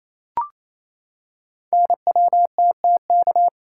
Morse code, heard as a keyed tone, about 28 wpm, 700 Hz. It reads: NWTTK
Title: Code Groups - 5 Letters - 28wpm